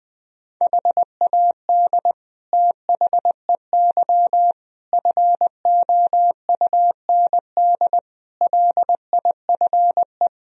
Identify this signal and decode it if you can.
Morse code, heard as a keyed tone, about 20 wpm, 700 Hz. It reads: HAD THEY FOUND LIFE
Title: Sentences from Top 1000 Words - Review Part 2 - 20wpm